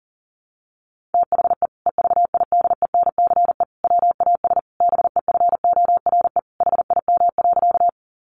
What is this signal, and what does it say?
Morse code, heard as a keyed tone, about 40 wpm, 700 Hz. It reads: THE EVIDENCE WAS BEFORE HIM.
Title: Sentences from Top 1300 Words 40wpm